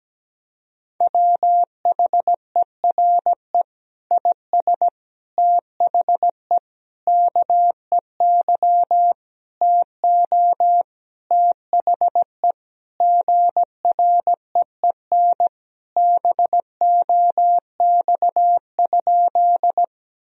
Morse code, heard as a keyed tone, about 17 wpm, 700 Hz. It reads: WHERE IS THE KEY TO THE GREEN BOX?
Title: Sentences from Top 1000 Words - Review Part 4 - 17wpm